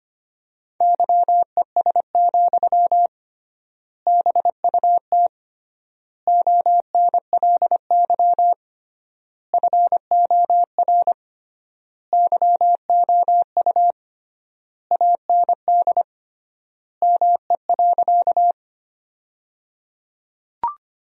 Morse code, heard as a keyed tone, about 25 wpm, 700 Hz. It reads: YES, BUT ONLY FOR YOU AND ME.
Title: 3x Word-Spacing - Sentences from Top 500 Words - 25wpm